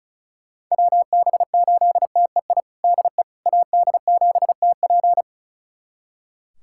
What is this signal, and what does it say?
Morse code, heard as a keyed tone, about 34 wpm, 700 Hz. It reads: WB8TEI DE AD7TP
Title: USA Callsign DE Callsign 35wpm